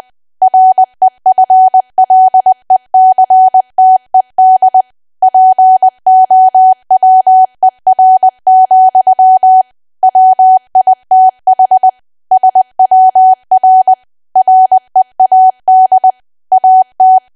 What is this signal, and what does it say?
Morse code, heard as a keyed tone, about 20 wpm, 750 Hz. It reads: REFLECTED POWER, WITH SWR READ AT
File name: cwa / W1AW-Level3 / 20WPM